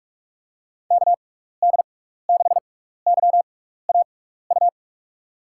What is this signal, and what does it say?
Morse code, heard as a keyed tone, about 46 wpm, spaced as for 16 wpm, 700 Hz. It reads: KD6YAU
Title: USA Call Signs - 6 Characters - 6x Character Spacing - 45wpm